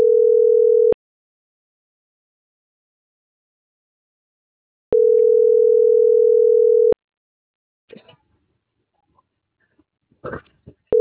Ringback tone (North America), 440 + 480 Hz.